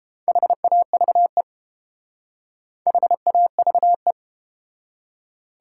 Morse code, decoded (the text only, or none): HAVE HAVE